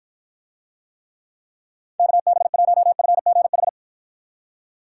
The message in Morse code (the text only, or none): KB1FGH